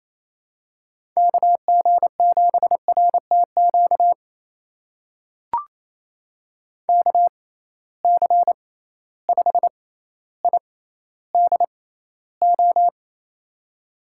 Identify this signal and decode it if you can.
Morse code, heard as a keyed tone, about 28 wpm, spaced as for 14 wpm, 700 Hz. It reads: KG7RTQ KC5IDO
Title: USA Call Signs - 6 Characters - 6x Character Spacing - 28wpm